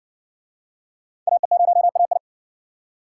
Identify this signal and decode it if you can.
Morse code, heard as a keyed tone, about 54 wpm, 700 Hz. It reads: AE0AI